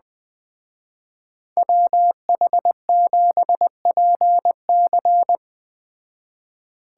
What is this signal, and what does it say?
Morse code, heard as a keyed tone, about 20 wpm, 700 Hz. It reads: WH7PC
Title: Speed Racing - USA 2x2 Call Signs 17wpm